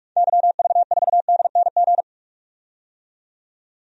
Morse code, heard as a keyed tone, about 46 wpm, 700 Hz. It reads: YV4DNG
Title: DXCC-100 Call Signs - 6 Characters - 45wpm